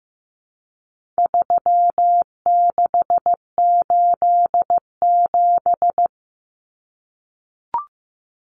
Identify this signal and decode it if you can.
Morse code, heard as a keyed tone, about 15 wpm, 700 Hz. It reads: 3687
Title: Numbers - 4-Digit 15wpm